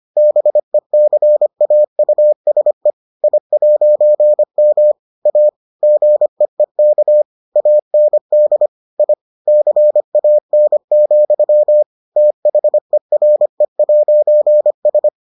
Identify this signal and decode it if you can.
Morse code, heard as a keyed tone, about 25 wpm, 600 Hz. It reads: BECAUSE I'M A GEEK AND I CAN, THERE'S